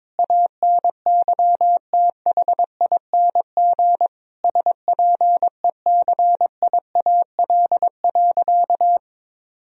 Morse code, heard as a keyed tone, about 22 wpm, 700 Hz. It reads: ANYTHING SPECIAL.
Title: Sentences from Top 1000 Words - Review Part 3 - 22wpm